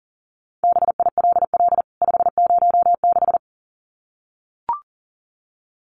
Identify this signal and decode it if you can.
Morse code, heard as a keyed tone, about 40 wpm, 700 Hz. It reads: BILL 506
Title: CWT Exchange 40wpm